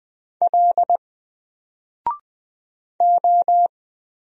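Morse code, heard as a keyed tone, about 20 wpm, 700 Hz. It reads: L O
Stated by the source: Single letter and number Morse drill at 20 wpm, from N4LCD.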